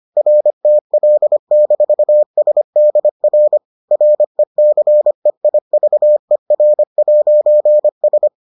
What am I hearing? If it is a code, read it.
Morse code, heard as a keyed tone, about 25 wpm, 600 Hz. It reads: RTL-SDR RECEIVER'S